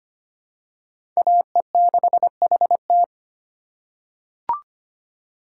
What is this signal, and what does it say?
Morse code, heard as a keyed tone, about 25 wpm, 700 Hz. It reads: AE6HT
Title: USA Call Signs - 5 Characters - 4x Character Spacing - 25wpm